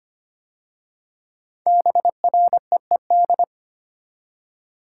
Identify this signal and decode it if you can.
Morse code, heard as a keyed tone, about 25 wpm, 700 Hz. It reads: BREED
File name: 5 Letter Words - First 3 Letters Given - 25wpm